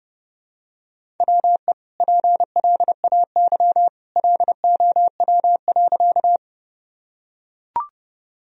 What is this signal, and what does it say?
Morse code, heard as a keyed tone, about 32 wpm, 700 Hz. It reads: WE PLAY LOW.
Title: Sentences from Top 1000 Words - Review Part 1 - 30wpm